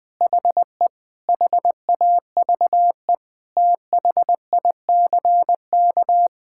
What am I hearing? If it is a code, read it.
Morse code, heard as a keyed tone, about 21 wpm, 700 Hz. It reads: HE HAVE THICK
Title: Sentences from Top 1600 Words 20wpm